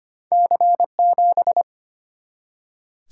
Morse code, heard as a keyed tone, about 25 wpm, 700 Hz. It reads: C7
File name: Two Letter-Number Combinations - 25wpm